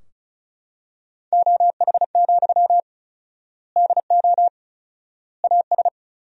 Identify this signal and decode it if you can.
Morse code, heard as a keyed tone, about 36 wpm, 700 Hz. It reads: OH, DO AS
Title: 4x Word-Spacing - Sentences from Top 500 Words - 35wpm